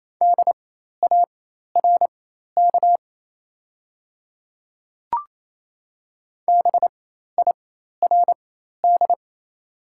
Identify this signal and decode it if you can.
Morse code, heard as a keyed tone, about 28 wpm, spaced as for 13 wpm, 690 Hz. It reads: DARK BIRD
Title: 4x Character Spacing - Top 500 Words - 28wpm